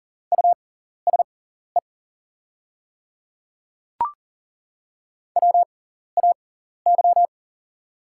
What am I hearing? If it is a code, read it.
Morse code, heard as a keyed tone, about 39 wpm, spaced as for 14 wpm, 700 Hz. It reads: USE WAY
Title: 6x Character Spacing - Top 100 Words - 40wpm